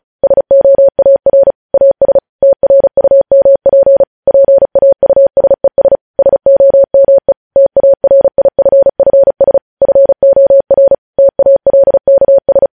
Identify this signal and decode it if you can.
Morse code, heard as a keyed tone, about 36 wpm, 550 Hz. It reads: SOAR AS TRUMP PAUSES SOME TARIFFS FOR TALKS
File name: podcast / News headlines-1x-35x35